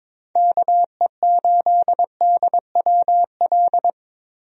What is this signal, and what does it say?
Morse code, heard as a keyed tone, about 22 wpm, 700 Hz. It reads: KE8DWL
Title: USA Call Signs - 6 Characters - Repeated 3x - 22wpm